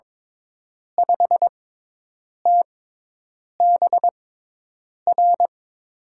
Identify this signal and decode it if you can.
Morse code, heard as a keyed tone, about 22 wpm, spaced as for 8 wpm, 700 Hz. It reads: 5TBR
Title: USA Call Signs - 6x Character Spacing - 22wpm